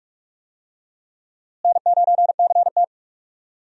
Morse code, heard as a keyed tone, about 42 wpm, 700 Hz. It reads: N9CT
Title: USA Call Signs - 4 Characters - Repeated 3x - 45wpm